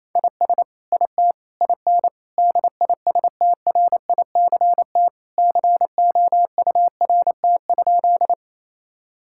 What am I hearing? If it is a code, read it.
Morse code, heard as a keyed tone, about 28 wpm, 700 Hz. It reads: IS IT IN DISTRICT COURT?
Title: Sentences from Top 1600 Words 28wpm